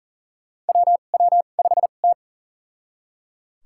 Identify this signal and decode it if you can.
Morse code, heard as a keyed tone, about 40 wpm, spaced as for 29 wpm, 700 Hz. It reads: WW5T
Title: USA Call Signs - 2x Character Spacing - 40wpm